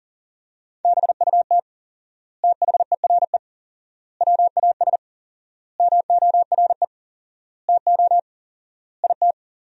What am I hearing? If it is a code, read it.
Morse code, heard as a keyed tone, about 40 wpm, 700 Hz. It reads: BUT THERE WAS MORE TO IT
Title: 4x Word-Spacing - Sentences from Top 100 Words - 40wpm